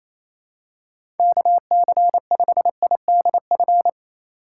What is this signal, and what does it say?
Morse code, heard as a keyed tone, about 28 wpm, 700 Hz. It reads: KC5IDF
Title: USA Call Signs - 6 Characters - 3x Character Spacing - 28wpm